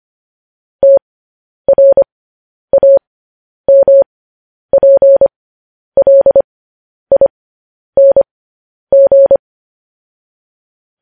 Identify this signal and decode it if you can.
Morse code, heard as a keyed tone, about 25 wpm, spaced as for 10 wpm, 550 Hz. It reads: TRAMPLING
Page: news headlines in morse code at 10 wpm